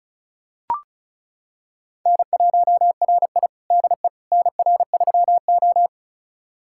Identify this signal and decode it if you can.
Morse code, heard as a keyed tone, about 36 wpm, 700 Hz. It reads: N1RI DE NR3O
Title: USA Callsign DE Callsign - 4 Characters - 35wpm